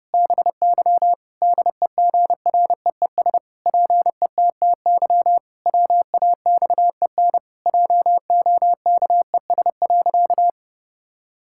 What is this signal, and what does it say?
Morse code, heard as a keyed tone, about 30 wpm, 700 Hz. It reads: BY DEGREES PETTY WAXEN JOKES.
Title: Pangram Challenge v2 - 30wpm